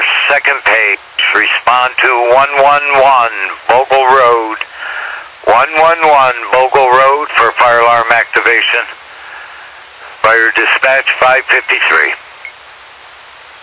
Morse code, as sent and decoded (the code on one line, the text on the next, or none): none